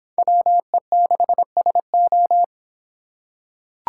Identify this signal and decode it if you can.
Morse code, heard as a keyed tone, about 25 wpm, 700 Hz. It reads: WE6SO